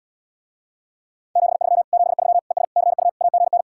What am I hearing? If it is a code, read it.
Morse code, heard as a keyed tone, about 80 wpm, 700 Hz. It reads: 73 73 ES CU AGN